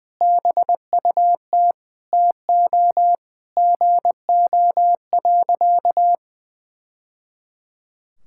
Morse code, heard as a keyed tone, about 20 wpm, 700 Hz. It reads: BUT TO GO.
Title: Sentences from Top 1000 Words - Review Part 6 - 20wpm